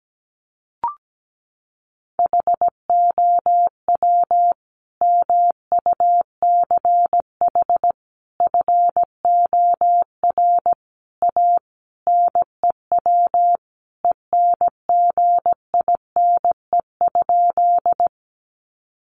Morse code, heard as a keyed tone, about 17 wpm, 700 Hz. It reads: HOW MUCH FOR A NEW ENGINE?
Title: Sentences from Top 1500 Words 17wpm